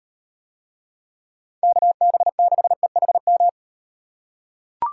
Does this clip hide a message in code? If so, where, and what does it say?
Yes: Morse code from 1.6 to 3.5 s, heard as a keyed tone, about 35 wpm, 700 Hz. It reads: KB6EHM